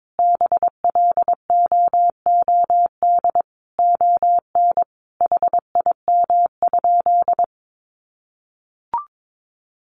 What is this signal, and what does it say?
Morse code, heard as a keyed tone, about 22 wpm, 700 Hz. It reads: BLOOD ON HIM?
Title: Sentences from Top 1100 Words 22wpm